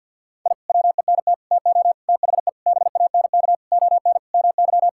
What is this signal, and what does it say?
Morse code, heard as a keyed tone, about 53 wpm, 700 Hz. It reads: I WENT TO THE BANK ON MY
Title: Sentences from Top 1000 Words - Review Part 6 - 50wpm